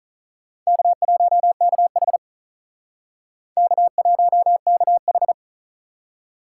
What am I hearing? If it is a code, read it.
Morse code, heard as a keyed tone, about 44 wpm, 700 Hz. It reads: K1KH K1KH